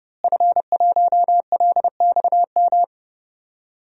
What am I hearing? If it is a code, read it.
Morse code, heard as a keyed tone, about 32 wpm, 700 Hz. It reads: F1LXM
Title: DXCC-100 Call Signs - 5 Characters - Repeated 2x - 30wpm